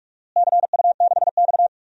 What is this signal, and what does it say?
Morse code, heard as a keyed tone, about 46 wpm, 700 Hz. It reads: CU6X
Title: DXCC-100 Call Signs - 4 Characters - Repeated 3x - 45wpm